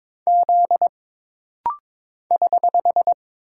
Morse code, heard as a keyed tone, about 22 wpm, 700 Hz. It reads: Z <HH>
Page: This rapid-fire timing is geared to push your ICR (Instant Character Recognition) skills.